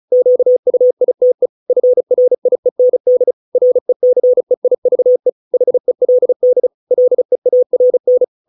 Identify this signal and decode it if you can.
Morse code, heard as a keyed tone, about 35 wpm, 500 Hz. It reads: QUITE FRIEND RECEIVE HELD LEARN